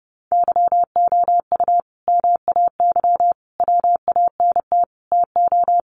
Morse code, heard as a keyed tone, about 30 wpm, 700 Hz. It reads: YOU MAY WANT TO
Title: Sentences from Top 700 Words 30wpm